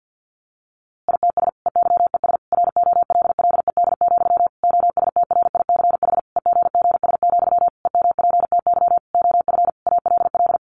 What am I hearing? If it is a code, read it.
Morse code, heard as a keyed tone, about 50 wpm, 700 Hz. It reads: ITS EYES ROLLED, OSTRICH EGGS, EMPTY OF ALL
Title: A Sound Of Thunder 50wpm